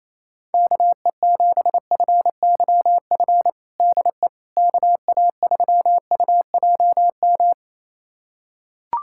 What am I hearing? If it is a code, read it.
Morse code, heard as a keyed tone, about 28 wpm, 700 Hz. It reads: KE7FYF DE KA3UJM
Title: USA Callsign DE Callsign - 6 Characters - 28wpm